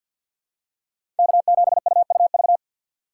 Morse code, heard as a keyed tone, about 48 wpm, 700 Hz. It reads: K7UUV
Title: USA Call Signs - 2x Character Spacing - 50wpm